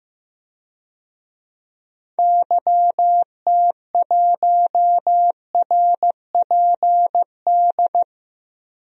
Morse code, heard as a keyed tone, about 15 wpm, 700 Hz. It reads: YT1RPD
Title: DXCC-100 Call Signs - 6 Characters - 15wpm